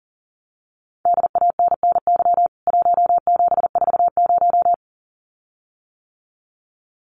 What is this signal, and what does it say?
Morse code, heard as a keyed tone, about 40 wpm, 700 Hz. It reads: DANNY 1740
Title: CWT Exchange 40wpm